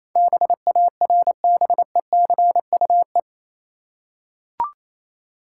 Morse code, heard as a keyed tone, about 27 wpm, 700 Hz. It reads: BARBECUE
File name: Easy Words - Speed-Racing with Hints - 22wpm